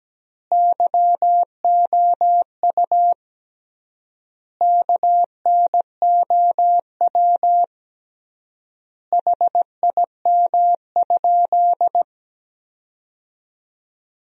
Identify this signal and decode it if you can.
Morse code, heard as a keyed tone, about 17 wpm, 700 Hz. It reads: YOU KNOW HIM?
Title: 3x Word-Spacing - Sentences from Top 100 Words - 17wpm